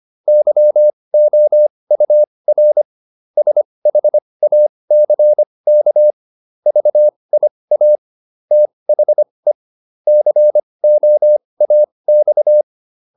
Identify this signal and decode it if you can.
Morse code, heard as a keyed tone, about 25 wpm, spaced as for 20 wpm, 600 Hz. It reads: YOUR SHACK VIA THE COAX